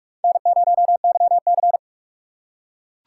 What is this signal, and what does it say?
Morse code, heard as a keyed tone, about 40 wpm, 700 Hz. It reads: N0YC